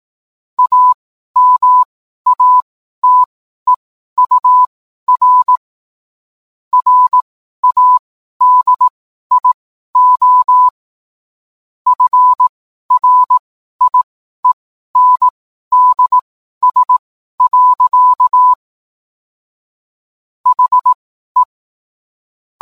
Morse code, heard as a keyed tone, about 18 wpm, spaced as for 12 wpm, 1000 Hz. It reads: AMATEUR RADIO FRIENDS. HE